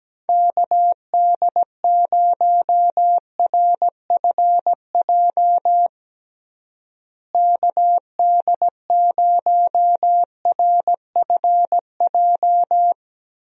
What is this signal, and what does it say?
Morse code, heard as a keyed tone, about 17 wpm, 700 Hz. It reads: KD0RFJ KD0RFJ
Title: USA Call Signs - 6 Characters - Repeated 3x - 17wpm